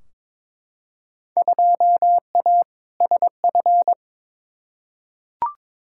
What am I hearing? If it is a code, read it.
Morse code, heard as a keyed tone, about 21 wpm, 700 Hz. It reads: 2A SF
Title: Field Day Exchanges - 22wpm